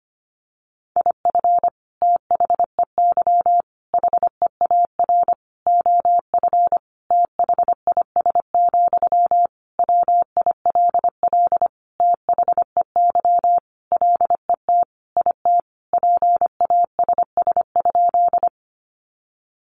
Morse code, heard as a keyed tone, about 25 wpm, 700 Hz. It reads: IF THEY HEAR OF THIS, WILL THEY LET IT PASS?
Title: Sentences from Top 400 Words 25wpm